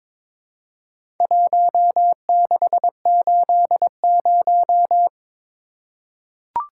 Morse code, heard as a keyed tone, about 22 wpm, 700 Hz. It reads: 1680
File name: practice / 22wpm / Four Letter-Number Combinations - 22wpm